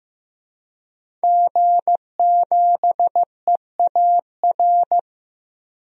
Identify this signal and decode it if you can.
Morse code, heard as a keyed tone, about 15 wpm, 700 Hz. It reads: G7EAR
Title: DXCC-100 Call Signs - 5 Characters - 15wpm